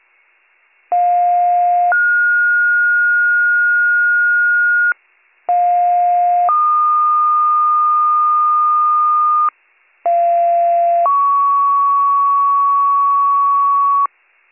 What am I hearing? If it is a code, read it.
Two-tone page, heard as a one-second tone, then a three-second tone, about 710 then 1430 Hz, about 710 then 1220 Hz, about 690 then 1120 Hz. It is heard from 0.9 to 14.1 s.